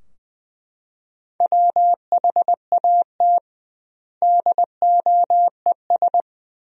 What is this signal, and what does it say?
Morse code, heard as a keyed tone, about 20 wpm, 700 Hz. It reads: WHAT DOES
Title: 2x Word-Spacing - Sentences from Top 200 Words - 20wpm